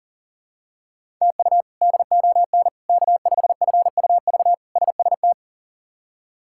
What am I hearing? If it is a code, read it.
Morse code, heard as a keyed tone, about 40 wpm, 700 Hz. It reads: TU DON K5FUV SST
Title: K1USN SST QSO Practice - 40wpm